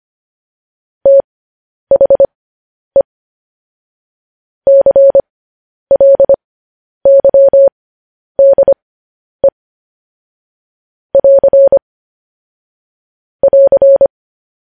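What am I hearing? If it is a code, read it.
Morse code, heard as a keyed tone, about 25 wpm, spaced as for 10 wpm, 550 Hz. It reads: THE CLYDE <AR> <AR>
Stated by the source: Morse code news headlines practice transmission.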